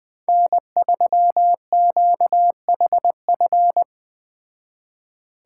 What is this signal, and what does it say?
Morse code, heard as a keyed tone, about 20 wpm, 700 Hz. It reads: N3QHF
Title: Speed Racing - USA 1x3 Call Signs 17wpm